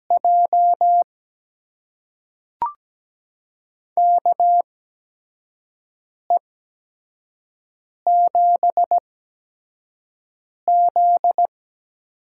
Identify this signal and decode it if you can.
Morse code, heard as a keyed tone, about 17 wpm, spaced as for 5 wpm, 700 Hz. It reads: J KE7Z
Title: USA Call Signs - 6 Characters - 8x Character Spacing - 17wpm